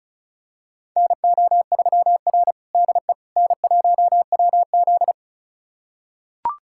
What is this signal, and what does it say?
Morse code, heard as a keyed tone, about 34 wpm, 700 Hz. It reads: NO3R DE N1WZ
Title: USA Callsign DE Callsign - 4 Characters - 35wpm